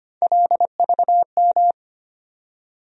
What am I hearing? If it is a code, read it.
Morse code, heard as a keyed tone, about 25 wpm, 700 Hz. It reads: LVM
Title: USA Call Sign Postfixes - 3 Letters 25wpm